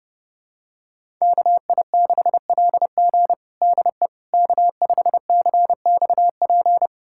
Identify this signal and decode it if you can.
Morse code, heard as a keyed tone, about 30 wpm, 700 Hz. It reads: KI6LG DE K5CXP
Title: USA Callsign DE Callsign - 5 Characters - 30wpm